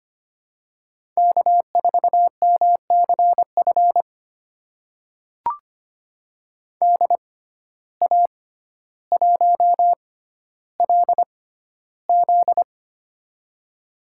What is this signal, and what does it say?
Morse code, heard as a keyed tone, about 25 wpm, spaced as for 13 wpm, 700 Hz. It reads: K4MCF DA1LZ
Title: USA Call Signs - 5 Characters - 6x Character Spacing - 25wpm